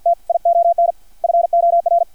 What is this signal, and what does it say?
Morse code, heard as a keyed tone, about 56 wpm, 690 Hz. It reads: TION FOR
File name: archived / CRSnet